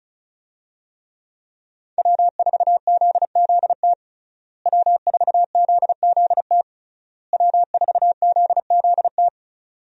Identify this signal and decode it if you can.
Morse code, heard as a keyed tone, about 36 wpm, 700 Hz. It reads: W4ZZT W4ZZT W4ZZT
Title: USA Call Signs - 5 Characters - Repeated 3x - 35wpm